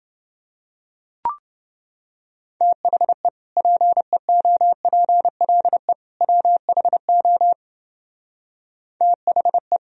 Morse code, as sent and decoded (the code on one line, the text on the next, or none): - .... / .--. --- .--. .-.. / .-- .... --- / - .... .
THE PEOPLE WHO THE